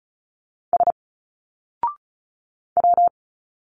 Morse code, heard as a keyed tone, about 34 wpm, 700 Hz. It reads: S W
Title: Single Letters 35wpm